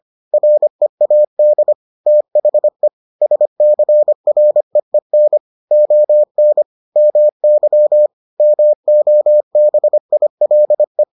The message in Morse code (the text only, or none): READ THE SCREEN ON MY MOBILE